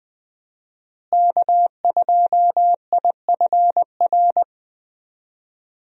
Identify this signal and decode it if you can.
Morse code, heard as a keyed tone, about 21 wpm, 700 Hz. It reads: K2IFR
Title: USA Call Signs - 5 Characters - Repeated 3x - 20wpm